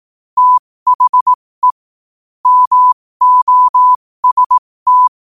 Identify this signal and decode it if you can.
Morse code, heard as a keyed tone, about 18 wpm, spaced as for 15 wpm, 1000 Hz. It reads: THE MOST
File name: qotdfiles / 15wpm